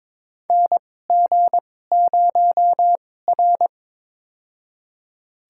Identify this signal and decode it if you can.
Morse code, heard as a keyed tone, about 21 wpm, spaced as for 16 wpm, 700 Hz. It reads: NG0R
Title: USA Call Signs - 2x Character Spacing - 22wpm